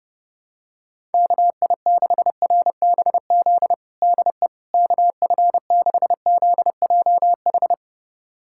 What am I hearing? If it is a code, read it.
Morse code, heard as a keyed tone, about 30 wpm, 700 Hz. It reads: KI6RBZ DE KF6ZJH
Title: USA Callsign DE Callsign - 6 Characters - 30wpm